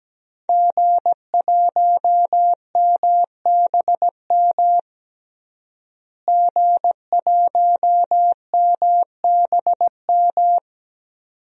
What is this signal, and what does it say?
Morse code, heard as a keyed tone, about 17 wpm, 700 Hz. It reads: G1MBM G1MBM
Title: DXCC-100 Call Signs - 5 Characters - Repeated 2x - 17wpm